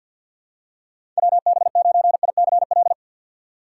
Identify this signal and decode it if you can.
Morse code, heard as a keyed tone, about 52 wpm, 700 Hz. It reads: WB9ICL
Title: USA Call Signs - 6x Character Spacing - 50wpm